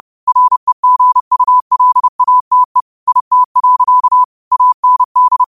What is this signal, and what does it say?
Morse code, heard as a keyed tone, about 30 wpm, 1000 Hz. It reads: REGULATE IT. AND